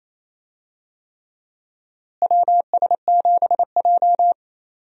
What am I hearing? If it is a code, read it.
Morse code, heard as a keyed tone, about 28 wpm, 700 Hz. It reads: WS7J
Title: USA Call Signs - 4 Characters - Repeated 3x - 28wpm